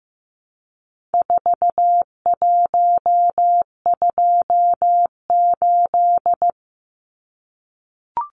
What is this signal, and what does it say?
Morse code, heard as a keyed tone, about 15 wpm, 700 Hz. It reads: 4128
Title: Numbers - 4-Digit 15wpm